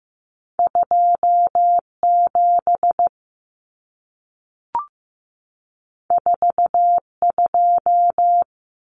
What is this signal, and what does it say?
Morse code, heard as a keyed tone, about 15 wpm, 700 Hz. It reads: 27 42
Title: Common Cut Numbers - 2-Digit 15wpm